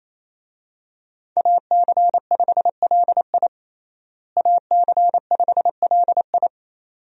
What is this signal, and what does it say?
Morse code, heard as a keyed tone, about 28 wpm, 700 Hz. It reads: AC5LI AC5LI
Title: USA Call Signs - 5 Characters - Repeated 2x - 28wpm